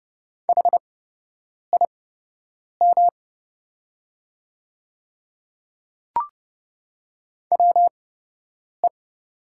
Morse code, heard as a keyed tone, about 30 wpm, spaced as for 8 wpm, 700 Hz. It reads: HIM WE
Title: 8x Character Spacing - Top 300 Words - 30wpm